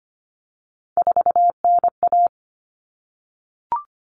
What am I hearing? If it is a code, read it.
Morse code, heard as a keyed tone, about 25 wpm, 700 Hz. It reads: 4NA
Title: Common Cut Numbers - 3-Digit 25wpm